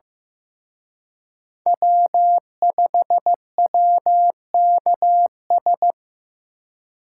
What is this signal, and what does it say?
Morse code, heard as a keyed tone, about 18 wpm, 700 Hz. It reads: W5WKS
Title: Speed Racing - USA 1x3 Call Signs 15wpm